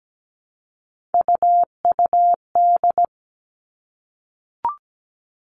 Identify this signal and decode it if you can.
Morse code, heard as a keyed tone, about 17 wpm, 700 Hz. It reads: UUD
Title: All Cut Numbers - 3-Digit 17wpm